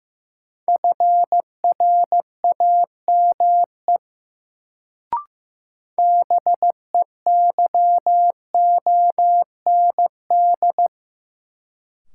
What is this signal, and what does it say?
Morse code, heard as a keyed tone, about 15 wpm, 700 Hz. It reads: FRAME BEYOND
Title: Sentences from Top 1000 Words - Review Part 6 - New Word Drill - 15wpm